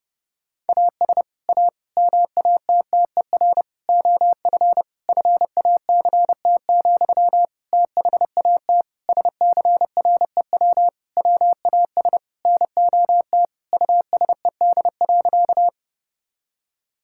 Morse code, heard as a keyed tone, about 30 wpm, 700 Hz. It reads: AS A MATTER OF FACT, THAT SCREW WAS NOT USED.
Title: Sentences from Top 1600 Words 30wpm